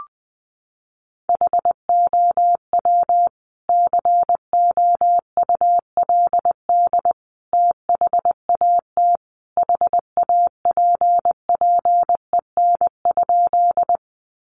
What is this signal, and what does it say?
Morse code, heard as a keyed tone, about 20 wpm, 700 Hz. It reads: HOW COULD THAT HAPPEN?
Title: Sentences from Top 400 Words 20wpm